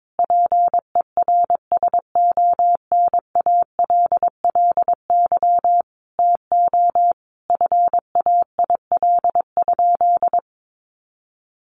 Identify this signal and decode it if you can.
Morse code, heard as a keyed tone, about 22 wpm, 700 Hz. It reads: PERSONALLY TO FAIL?
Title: Sentences from Top 1400 Words 22wpm